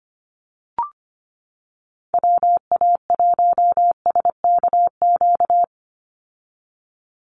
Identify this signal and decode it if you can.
Morse code, heard as a keyed tone, about 25 wpm, 700 Hz. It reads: WA1SKQ
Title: Call Signs - 25wpm